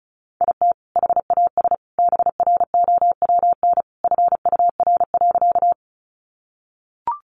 Morse code, heard as a keyed tone, about 34 wpm, 700 Hz. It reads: IT HAS BROWN FUR.
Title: Easy Phrases 35wpm